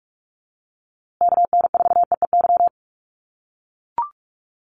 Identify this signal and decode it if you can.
Morse code, heard as a keyed tone, about 44 wpm, 700 Hz. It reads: KN4EEY
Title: USA Call Signs Repeated 3x 45wpm